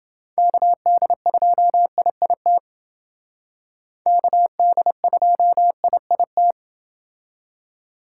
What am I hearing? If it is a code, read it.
Morse code, heard as a keyed tone, about 30 wpm, 700 Hz. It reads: KD2IIT KD2IIT